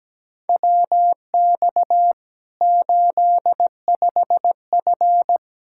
Morse code, heard as a keyed tone, about 17 wpm, 700 Hz. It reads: WX 85F
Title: Practice QSO - Exchange 2 - Info - 17wpm